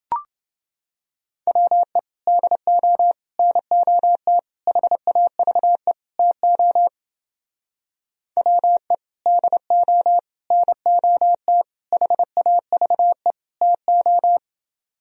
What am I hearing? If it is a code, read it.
Morse code, heard as a keyed tone, about 30 wpm, 700 Hz. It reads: WE DO NOT HAVE TO WE DO NOT HAVE TO